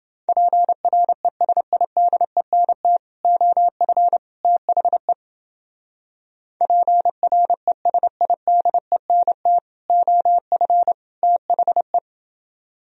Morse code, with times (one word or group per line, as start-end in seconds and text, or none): 0.3-3.0 PRESIDENT
3.2-4.2 OF
4.4-5.1 THE
6.6-9.6 PRESIDENT
9.9-10.9 OF
11.2-12.0 THE